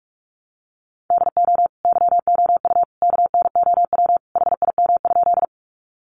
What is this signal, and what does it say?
Morse code, heard as a keyed tone, about 44 wpm, 700 Hz. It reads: DO YOU KNOW HIM?
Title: Sentences from Top 100 Words 45wpm